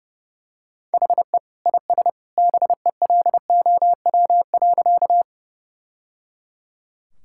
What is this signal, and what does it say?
Morse code, heard as a keyed tone, about 30 wpm, 700 Hz. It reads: HE IS BELOW.
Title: Sentences from Top 1000 Words - Review Part 5 - 30wpm